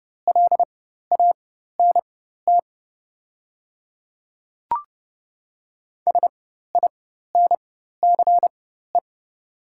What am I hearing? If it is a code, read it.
Morse code, heard as a keyed tone, about 30 wpm, spaced as for 14 wpm, 700 Hz. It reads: LANT SINCE
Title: 4x Character Spacing - Top 500 Words - 30wpm